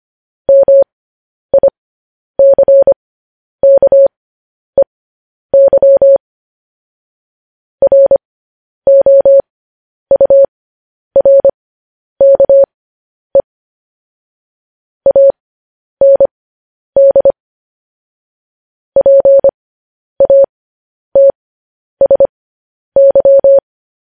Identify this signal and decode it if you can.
Morse code, heard as a keyed tone, about 25 wpm, spaced as for 10 wpm, 550 Hz. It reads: MICKEY ROURKE AND PATSY
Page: news headlines in morse code at 10 wpm